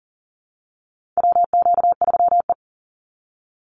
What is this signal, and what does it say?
Morse code, heard as a keyed tone, about 40 wpm, 700 Hz. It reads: WQ3E
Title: Call Signs Repeated 3x - 40wpm